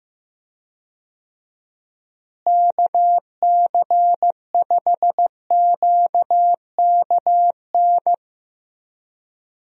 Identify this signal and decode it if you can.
Morse code, heard as a keyed tone, about 15 wpm, 700 Hz. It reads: KC5QKN
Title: USA Call Signs - 6 Characters - Repeated 3x - 15wpm